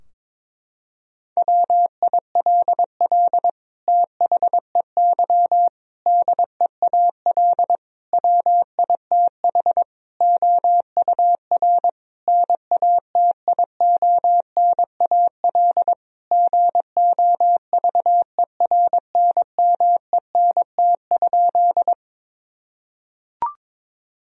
Morse code, heard as a keyed tone, about 21 wpm, 700 Hz. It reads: WILL THEY DEAL WITH OUR NATIONAL GOVERNMENT?
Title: Sentences from Top 1000 Words - Review Part 4 - 22wpm